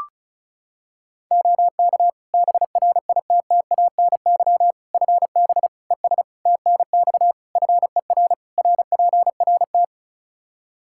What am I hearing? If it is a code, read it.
Morse code, heard as a keyed tone, about 36 wpm, 700 Hz. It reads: OK BRITTANY FB ES TNX FER RPRT
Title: K1USN SST QSO Practice 35wpm